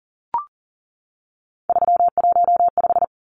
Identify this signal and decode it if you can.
Morse code, heard as a keyed tone, about 40 wpm, 700 Hz. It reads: 315
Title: Numbers - 3-Digit 40wpm